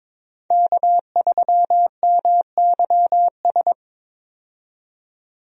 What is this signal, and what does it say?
Morse code, heard as a keyed tone, about 21 wpm, 700 Hz. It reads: K3MYS
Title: USA Call Signs - 5 Characters - 22wpm